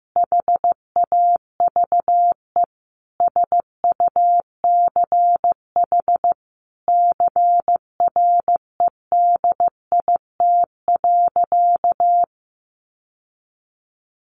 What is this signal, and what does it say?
Morse code, heard as a keyed tone, about 14 wpm, 700 Hz. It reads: HAVE SUCH CREDIT.
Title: Sentences from Top 1000 Words 15wpm